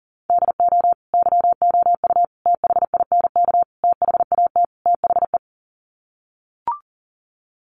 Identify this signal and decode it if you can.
Morse code, heard as a keyed tone, about 37 wpm, 700 Hz. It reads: DO YOU THINK THAT THE
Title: Sets of 5 Words Encore 40wpm